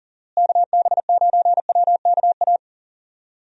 Morse code, heard as a keyed tone, about 40 wpm, 700 Hz. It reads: KB9WKA
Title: USA Call Signs - 6 Characters - Repeated 3x - 40wpm